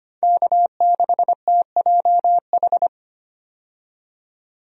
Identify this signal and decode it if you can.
Morse code, heard as a keyed tone, about 25 wpm, 700 Hz. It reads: K6TJH
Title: USA Call Signs - 5 Characters - 25wpm